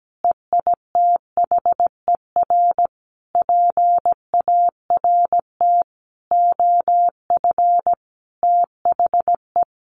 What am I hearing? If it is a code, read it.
Morse code, heard as a keyed tone, about 17 wpm, 700 Hz. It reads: EITHER PART OF THE
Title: Sentences from Top 1500 Words 17wpm